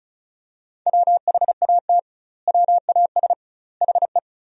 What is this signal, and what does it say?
Morse code, heard as a keyed tone, about 34 wpm, 700 Hz. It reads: WHAT WAS HE
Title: 2x Word-Spacing - Sentences from Top 100 Words - 35wpm